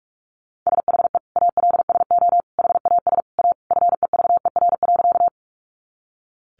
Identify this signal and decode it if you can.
Morse code, heard as a keyed tone, about 44 wpm, 700 Hz. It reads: SHE ALSO HAS A FEVER.
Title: Easy Phrases 45wpm